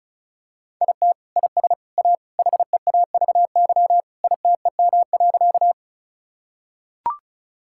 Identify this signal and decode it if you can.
Morse code, heard as a keyed tone, about 36 wpm, 700 Hz. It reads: IT IS A HEAVY ITEM.
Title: Sentences from Top 1000 Words - Review Part 6 - 35wpm